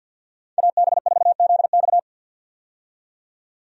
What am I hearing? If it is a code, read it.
Morse code, heard as a keyed tone, about 52 wpm, 700 Hz. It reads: AB4ZX
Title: USA Call Signs - 5 Characters - 50wpm